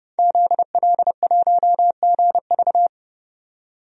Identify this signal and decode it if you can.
Morse code, heard as a keyed tone, about 29 wpm, 700 Hz. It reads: ZL1GV
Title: DXCC-100 Call Signs - 5 Characters - 30wpm